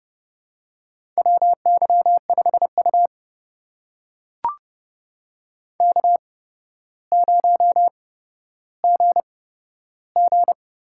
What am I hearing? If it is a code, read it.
Morse code, heard as a keyed tone, about 30 wpm, spaced as for 13 wpm, 700 Hz. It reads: WY5U K0GG
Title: USA Call Signs - 4 Characters - 8x Character Spacing - 30wpm